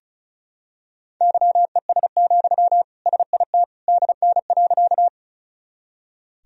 Morse code, heard as a keyed tone, about 36 wpm, 700 Hz. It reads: YES, SIT DN.
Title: Sentences from Top 500 Words and Abbreviations 35wpm